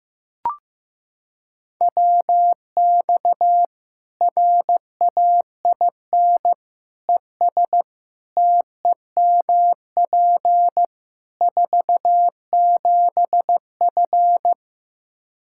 Morse code, heard as a keyed tone, about 15 wpm, 700 Hz. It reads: WX RAIN ES TEMP 47F
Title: Practice QSO - Exchange 2 - Info - 15wpm